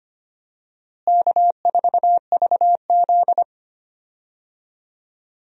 Morse code, heard as a keyed tone, about 25 wpm, 700 Hz. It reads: K4VZ
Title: USA Call Signs - 4 Characters - Repeated 3x - 25wpm